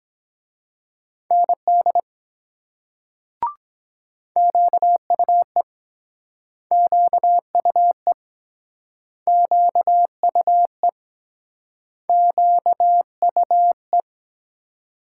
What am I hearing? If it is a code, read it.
Morse code, heard as a keyed tone, about 23 wpm, 700 Hz. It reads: ND QUE QUE QUE QUE